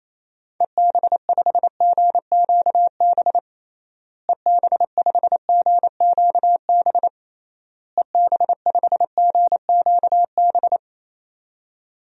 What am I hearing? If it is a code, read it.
Morse code, heard as a keyed tone, about 28 wpm, 700 Hz. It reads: EB5GQB EB5GQB EB5GQB
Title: DXCC-100 Call Signs - 6 Characters - Repeated 3x - 28wpm